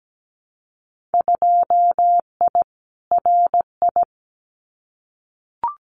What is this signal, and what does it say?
Morse code, heard as a keyed tone, about 17 wpm, 700 Hz. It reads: 2I RI
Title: Winter Field Day Exchanges Encore 17wpm